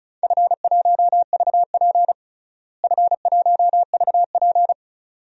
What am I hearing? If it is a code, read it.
Morse code, heard as a keyed tone, about 34 wpm, 700 Hz. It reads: F1VP F1VP
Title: DXCC-100 Call Signs - 4 Characters - Repeated 2x - 35wpm